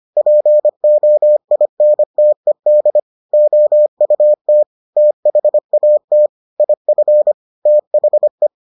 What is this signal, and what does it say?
Morse code, heard as a keyed tone, about 25 wpm, 600 Hz. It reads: POINTED OUT THAT IF THE